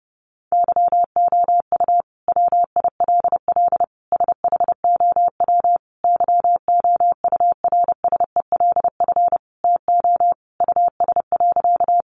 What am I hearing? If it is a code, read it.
Morse code, heard as a keyed tone, about 30 wpm, 700 Hz. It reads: YOU WILL SHOW YOURSELF TO US.
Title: Sentences from Top 600 Words 30wpm